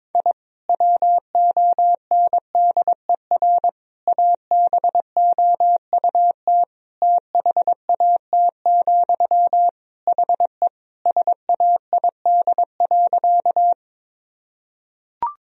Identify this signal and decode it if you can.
Morse code, heard as a keyed tone, about 22 wpm, 700 Hz. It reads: I WONDER ABOUT THAT, HE SAID.
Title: Sentences from Top 1000 Words - Review Part 3 - 22wpm